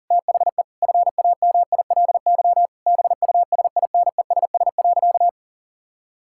Morse code, heard as a keyed tone, about 37 wpm, 700 Hz. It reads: THE FAMILY BUSINESS.
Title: Sentences from Top 1000 Words - Review Part 6 - 40wpm